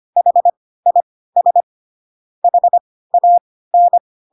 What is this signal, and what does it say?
Morse code, heard as a keyed tone, about 25 wpm, spaced as for 16 wpm, 700 Hz. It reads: HIS HAN
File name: Pinocchio / Pinocchio ch3 10 25 ik1hge